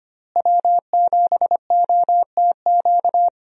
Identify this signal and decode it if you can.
Morse code, heard as a keyed tone, about 25 wpm, 690 Hz. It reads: W7OTQ
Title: USA Call Signs - 5 Characters - Repeated 2x - 25wpm